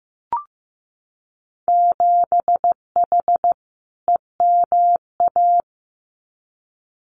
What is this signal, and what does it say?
Morse code, heard as a keyed tone, about 15 wpm, 700 Hz. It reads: 7H EMA
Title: Winter Field Day Exchanges Encore 15wpm